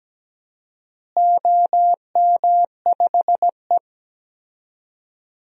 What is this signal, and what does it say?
Morse code, heard as a keyed tone, about 17 wpm, 700 Hz. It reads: OM5E
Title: DXCC-100 Call Signs - 4 Characters - Repeated 2x - 17wpm